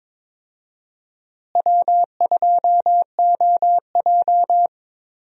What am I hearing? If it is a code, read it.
Morse code, heard as a keyed tone, about 22 wpm, 700 Hz. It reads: W2OJ
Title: USA Call Signs - 4 Characters - Repeated 3x - 22wpm